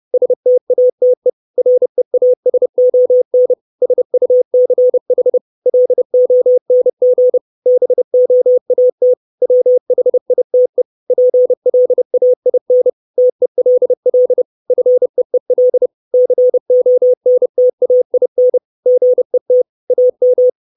Morse code, heard as a keyed tone, about 30 wpm, 500 Hz. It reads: STATE REASON SUCH LONG BOAT WHITE PLAIN TELL FEEL CONTAIN GET AM
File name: 632 Words 30WPM